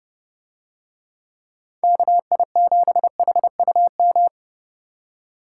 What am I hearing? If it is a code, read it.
Morse code, heard as a keyed tone, about 29 wpm, 700 Hz. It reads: KI7HUM